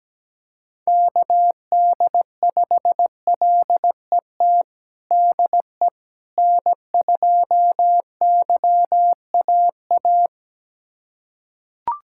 Morse code, heard as a keyed tone, about 17 wpm, 700 Hz. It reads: KD5LET DE N2YAA
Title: USA Callsign DE Callsign 17wpm